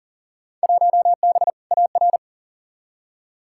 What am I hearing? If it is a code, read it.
Morse code, heard as a keyed tone, about 40 wpm, 700 Hz. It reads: 1B AR
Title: Field Day Exchanges Encore - 40wpm